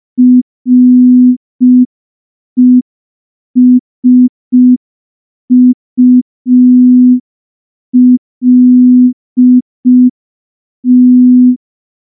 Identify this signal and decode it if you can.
Morse code, heard as a keyed tone, about 5 wpm, 250 Hz. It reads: RESULT
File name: morse / morse-training-05WPM